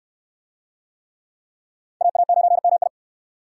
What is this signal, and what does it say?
Morse code, heard as a keyed tone, about 69 wpm, 700 Hz. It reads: AA9MI